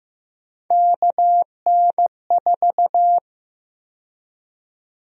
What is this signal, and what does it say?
Morse code, heard as a keyed tone, about 15 wpm, 700 Hz. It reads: KN4
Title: USA Call Sign Prefixes - 15wpm